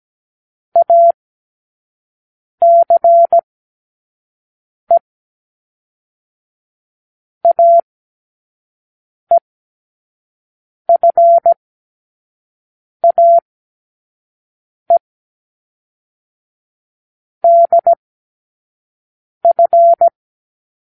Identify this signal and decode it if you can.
Morse code, heard as a keyed tone, about 17 wpm, 680 Hz. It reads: A C E A E F A E D F
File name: k7qo / Lesson9(A-F)